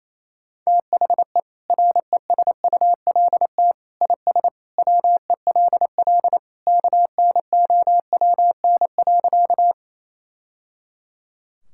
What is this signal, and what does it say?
Morse code, heard as a keyed tone, about 28 wpm, 700 Hz. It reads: THE RESULT IS WELL KNOWN.
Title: Sentences from Top 1000 Words - Review Part 5 - 28wpm